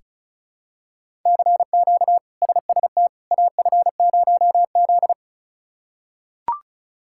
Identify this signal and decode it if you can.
Morse code, heard as a keyed tone, about 36 wpm, 700 Hz. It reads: CQ SST AF0Z
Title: K1USN SST QSO Practice - 35wpm